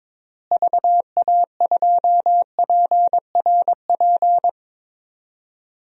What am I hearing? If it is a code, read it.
Morse code, heard as a keyed tone, about 22 wpm, 700 Hz. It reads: VA2PRP
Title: DXCC-100 Call Signs - 6 Characters - Repeated 2x - 22wpm